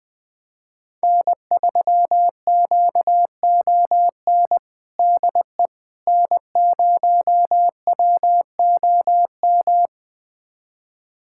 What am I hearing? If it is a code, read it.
Morse code, heard as a keyed tone, about 20 wpm, 700 Hz. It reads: N3QON DE N0WOM
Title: USA Callsign DE Callsign - 5 Characters - 20wpm